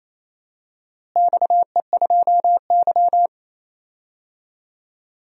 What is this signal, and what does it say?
Morse code, heard as a keyed tone, about 28 wpm, 700 Hz. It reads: XE2Y
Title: DXCC-100 Call Signs - 4 Characters - Repeated 3x - 28wpm